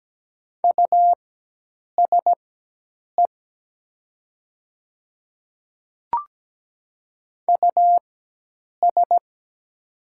Morse code, heard as a keyed tone, about 17 wpm, spaced as for 8 wpm, 700 Hz. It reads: USE US
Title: 4x Character Spacing - Top 300 Words - 17wpm